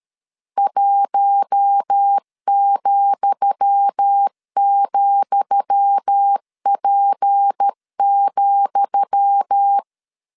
Morse code, heard as a keyed tone, about 13 wpm, 800 Hz. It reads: 1,,P,
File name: LEZIONI / 10 SIGNS / 10D COMA 12wpm 2544